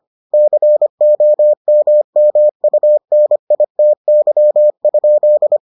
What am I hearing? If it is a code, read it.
Morse code, heard as a keyed tone, about 25 wpm, 600 Hz. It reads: COMMUNITY?